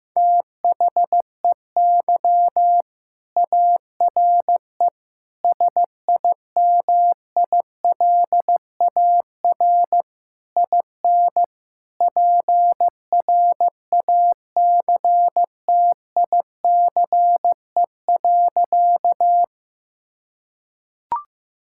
Morse code, heard as a keyed tone, about 15 wpm, 700 Hz. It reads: THEY ARE SIMILAR IN PRACTICE.
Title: Sentences from Top 1000 Words - Review Part 5 - 15wpm